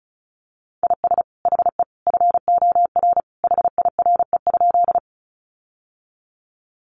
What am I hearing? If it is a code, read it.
Morse code, heard as a keyed tone, about 34 wpm, 700 Hz. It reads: IS HE FOR HIRE?
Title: Sentences from Top 1400 Words 35wpm